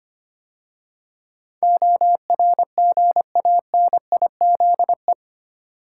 Morse code, heard as a keyed tone, about 25 wpm, 700 Hz. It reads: ORGANIZE
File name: 8 Letter Words - First 3 Letters Given - 25wpm